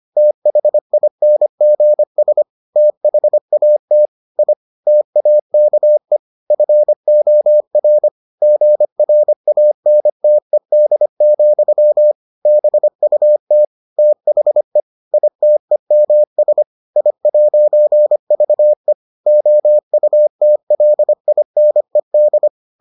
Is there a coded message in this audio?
Yes: Morse code from 0.2 to 22.5 s, heard as a keyed tone, about 25 wpm, 600 Hz. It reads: THINGS THAT I TAKE FOR GRANTED, BUT THE ITEMS I'VE OUTLINED